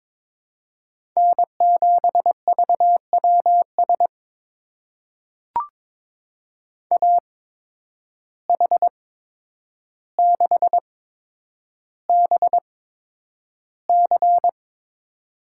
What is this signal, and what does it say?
Morse code, heard as a keyed tone, about 22 wpm, spaced as for 9 wpm, 700 Hz. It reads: N7VWS AH6BC
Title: USA Call Signs - 5 Characters - 8x Character Spacing - 22wpm